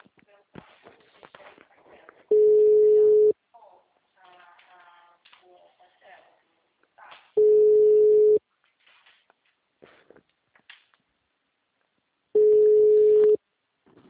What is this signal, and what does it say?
Ringback tone (Europe, 425 Hz), with speech.
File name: out- -2012- - - .13661